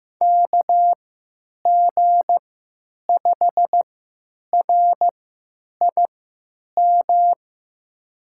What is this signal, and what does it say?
Morse code, heard as a keyed tone, about 15 wpm, spaced as for 9 wpm, 700 Hz. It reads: KG5RIM
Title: USA Call Signs - 3x Character Spacing - 15wpm